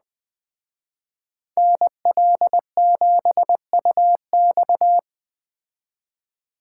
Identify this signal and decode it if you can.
Morse code, heard as a keyed tone, about 21 wpm, 700 Hz. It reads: NL7UX